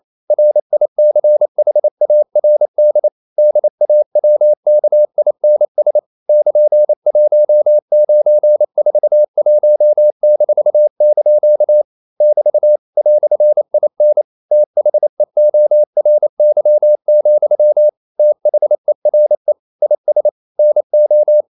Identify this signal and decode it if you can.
Morse code, heard as a keyed tone, about 28 wpm, 600 Hz. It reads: RICHARD DAWKINS <KN>1941-) <BT> "IN THEORY, THERE IS NO
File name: CW / Practice / EnglishQuotations0128